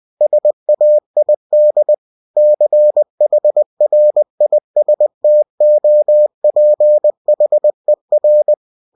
Morse code, heard as a keyed tone, about 20 wpm, 600 Hz. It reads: SAID CHRISTOPHER
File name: cw-books / winnie-the-pooh / winnie-20wpm-page-0081